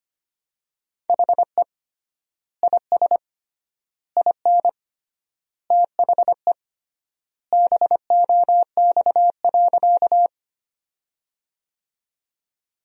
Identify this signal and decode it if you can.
Morse code, heard as a keyed tone, about 25 wpm, 700 Hz. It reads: HE IS IN THE BOX.
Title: 3x Word-Spacing - Sentences from Top 500 Words - 25wpm